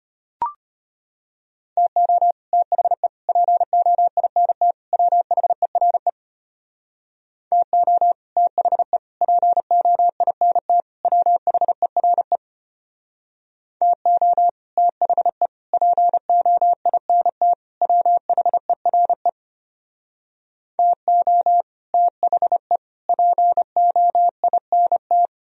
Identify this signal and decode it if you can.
Morse code, heard as a keyed tone, about 30 wpm, 700 Hz. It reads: TO THE POINT WHERE TO THE POINT WHERE TO THE POINT WHERE TO THE POINT
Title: Speed-Racing - Sets of 4 Words 25wpm